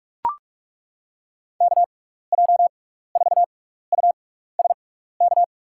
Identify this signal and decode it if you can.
Morse code, heard as a keyed tone, about 41 wpm, spaced as for 15 wpm, 700 Hz. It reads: KJ4USK
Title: USA Call Signs - 6 Characters - 6x Character Spacing - 45wpm